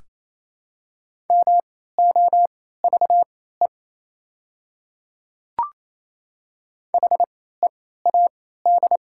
Morse code, heard as a keyed tone, about 28 wpm, spaced as for 16 wpm, 700 Hz. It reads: MOVE HEAD
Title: 3x Character Spacing - Top 300 Words - 28wpm